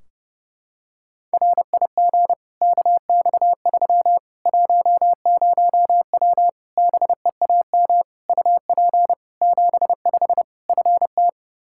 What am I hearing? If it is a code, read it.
Morse code, heard as a keyed tone, about 32 wpm, 700 Hz. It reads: RIG KX3 10W BEAM UP 75 FT
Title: Practice QSO - Exchange 2 - Info - 30wpm